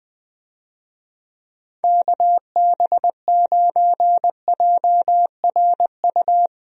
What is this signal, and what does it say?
Morse code, heard as a keyed tone, about 20 wpm, 700 Hz. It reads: KB9JRU
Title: USA Call Signs - 6 Characters - Repeated 2x - 20wpm